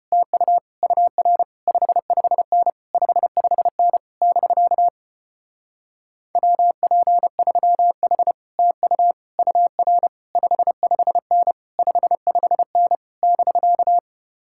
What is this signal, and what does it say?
Morse code, heard as a keyed tone, about 33 wpm, 700 Hz. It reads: TU UR 55N 55N <BK> WP3H TU UR 55N 55N <BK>